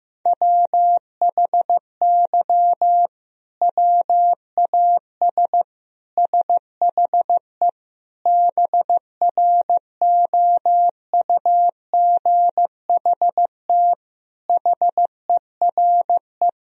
Morse code, heard as a keyed tone, about 16 wpm, 700 Hz. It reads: WHY WAS SHE BROUGHT HERE
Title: Sentences from Top 1000 Words - Review Part 3 - 15wpm